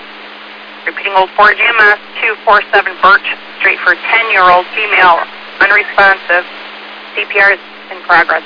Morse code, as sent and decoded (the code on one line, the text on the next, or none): none